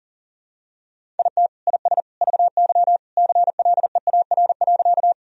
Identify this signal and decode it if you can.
Morse code, heard as a keyed tone, about 40 wpm, 700 Hz. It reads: IT IS VY CLEAR.
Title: Sentences from Top 500 Words and Abbreviations 40wpm